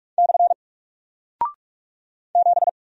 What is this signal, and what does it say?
Morse code, heard as a keyed tone, about 44 wpm, 700 Hz. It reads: / 7
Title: Single Character (Letter:Number:Punctuation:Prosign) 45wpm